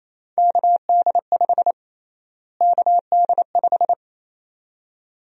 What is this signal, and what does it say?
Morse code, heard as a keyed tone, about 28 wpm, 700 Hz. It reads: KD5 KD5
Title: USA Call Sign Prefixes - Repeated 2x - 28wpm